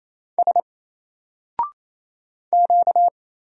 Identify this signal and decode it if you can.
Morse code, heard as a keyed tone, about 27 wpm, 700 Hz. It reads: S Q
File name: Single-Letter-Number-28wpm